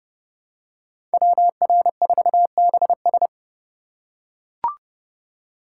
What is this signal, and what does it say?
Morse code, heard as a keyed tone, about 30 wpm, 700 Hz. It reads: WR4BS
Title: USA Call Signs - 5 Characters - 4x Character Spacing - 30wpm